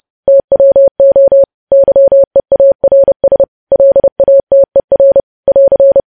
Morse code, heard as a keyed tone, about 30 wpm, 550 Hz. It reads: TWO YEARS LATER <AR>
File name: podcast / News headlines-1x-30x30